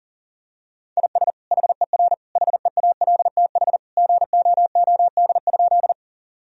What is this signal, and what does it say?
Morse code, heard as a keyed tone, about 37 wpm, 700 Hz. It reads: IS HER HEALTH GOOD?
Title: Sentences from Top 1000 Words - Review Part 3 - 40wpm